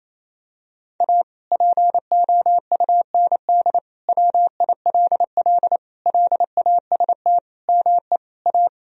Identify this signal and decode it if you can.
Morse code, heard as a keyed tone, about 28 wpm, 700 Hz. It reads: A POUND WILL LAST ME A
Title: Sentences from Top 1000 Words - Review Part 2 - 28wpm